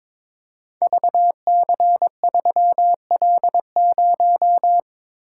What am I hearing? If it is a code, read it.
Morse code, heard as a keyed tone, about 22 wpm, 700 Hz. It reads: VC3L0
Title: Code Groups - 5 Letter:Numbers - 22wpm